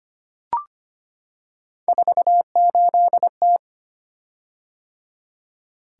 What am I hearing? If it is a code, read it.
Morse code, heard as a keyed tone, about 25 wpm, 700 Hz. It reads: 48T
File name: Three Letter-Number Combinations - 25wpm